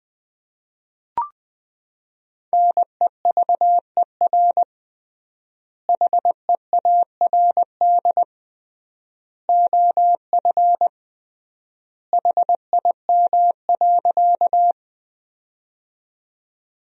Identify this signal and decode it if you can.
Morse code, heard as a keyed tone, about 21 wpm, 700 Hz. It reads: NEVER HEARD OF HIM.
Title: 3x Word-Spacing - Sentences from Top 400 Words - 20wpm